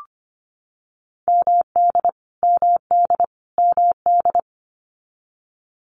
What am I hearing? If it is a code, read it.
Morse code, heard as a keyed tone, about 25 wpm, 700 Hz. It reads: MD MD MD
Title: US States Repeated 3x 25wpm